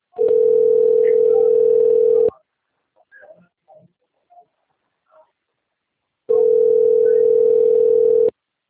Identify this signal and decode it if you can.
Ringback tone (North America), 440 + 480 Hz.